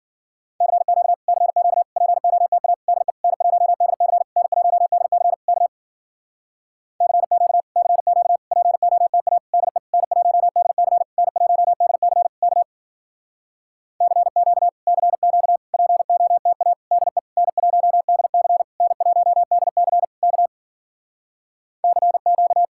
Morse code, heard as a keyed tone, about 59 wpm, 700 Hz. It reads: CQ CQ POTA DE N1DC N1DC K CQ CQ POTA DE N1DC N1DC K CQ CQ POTA DE N1DC N1DC K CQ